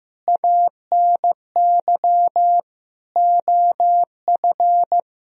Morse code, heard as a keyed tone, about 15 wpm, 700 Hz. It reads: ANY OF
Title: Sentences from Top 1000 Words - Review Part 1 - 15wpm